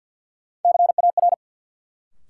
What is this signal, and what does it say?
Morse code, heard as a keyed tone, about 49 wpm, 700 Hz. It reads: CAR
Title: Sets of 5 Words Encore - New Word Drill - 50wpm